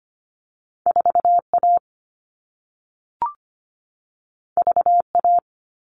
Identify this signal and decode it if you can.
Morse code, heard as a keyed tone, about 25 wpm, 700 Hz. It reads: 4A VA
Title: All Cut Numbers - 2-Digit 25wpm